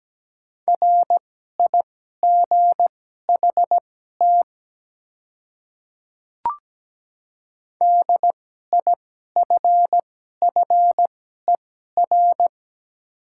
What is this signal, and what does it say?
Morse code, heard as a keyed tone, about 17 wpm, spaced as for 12 wpm, 700 Hz. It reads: RIGHT DIFFER
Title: 2x Character Spacing - Top 200 Words - 17wpm